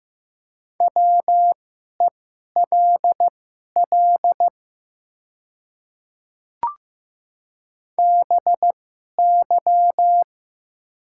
Morse code, heard as a keyed tone, about 15 wpm, spaced as for 11 wpm, 700 Hz. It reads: WELL BY
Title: 2x Character Spacing - Top 200 Words - 15wpm